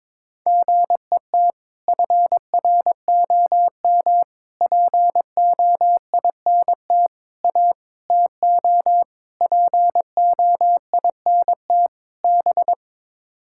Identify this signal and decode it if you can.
Morse code, heard as a keyed tone, about 22 wpm, 700 Hz. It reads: GET FROM POINT A TO POINT B
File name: practice / 22wpm / Sets of 7 Words - 22wpm